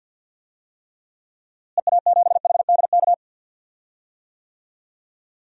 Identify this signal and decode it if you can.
Morse code, heard as a keyed tone, about 52 wpm, 700 Hz. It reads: EA7HDK
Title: DXCC-100 Call Signs - 6 Characters - Repeated 3x - 50wpm